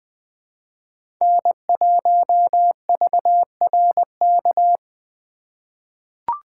Morse code, heard as a keyed tone, about 20 wpm, 700 Hz. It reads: N1VRK
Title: USA Call Signs - 3x Character Spacing - 20wpm